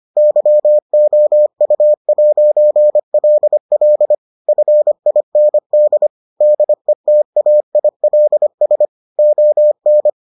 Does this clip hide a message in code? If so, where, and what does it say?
Yes: Morse code from 0.2 to 10.1 s, heard as a keyed tone, about 25 wpm, 600 Hz. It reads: YOU'LL FIND DETAILS ON